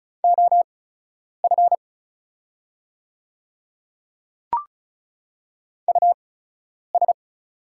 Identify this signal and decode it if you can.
Morse code, heard as a keyed tone, about 35 wpm, spaced as for 10 wpm, 690 Hz. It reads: OF US
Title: 8x Character Spacing - Top 300 Words - 35wpm